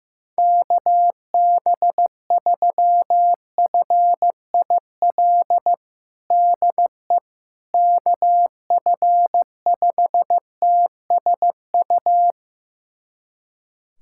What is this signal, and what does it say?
Morse code, heard as a keyed tone, about 15 wpm, 700 Hz. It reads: KB3FIL DE KF5TSU
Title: USA Callsign DE Callsign - 6 Characters - 15wpm